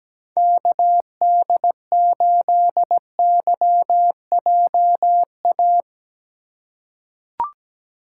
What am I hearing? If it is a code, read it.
Morse code, heard as a keyed tone, about 17 wpm, 700 Hz. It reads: KD8YJA
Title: USA Call Signs - 6 Characters - 3x Character Spacing - 17wpm